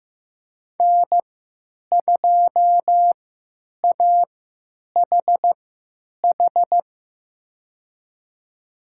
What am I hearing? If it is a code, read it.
Morse code, heard as a keyed tone, about 15 wpm, spaced as for 9 wpm, 700 Hz. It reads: N2AHH
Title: USA Call Signs - 5 Characters - 3x Character Spacing - 15wpm